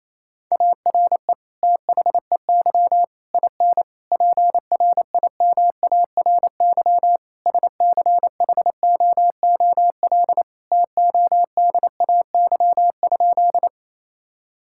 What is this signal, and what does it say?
Morse code, heard as a keyed tone, about 28 wpm, 700 Hz. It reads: ARE THEY IN PRIMARY SCHOOL TODAY?
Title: Sentences from Top 1000 Words - Review Part 5 - 28wpm